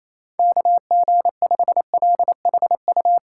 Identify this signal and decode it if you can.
Morse code, heard as a keyed tone, about 28 wpm, 700 Hz. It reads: KG5LHU
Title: USA Callsign DE Callsign - 6 Characters - 28wpm